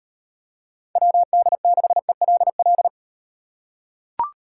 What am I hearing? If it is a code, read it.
Morse code, heard as a keyed tone, about 38 wpm, 700 Hz. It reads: WD6ELL